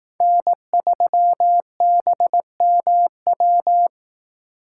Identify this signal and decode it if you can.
Morse code, heard as a keyed tone, about 18 wpm, 700 Hz. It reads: N3BMW
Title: Speed Racing - USA 1x3 Call Signs 15wpm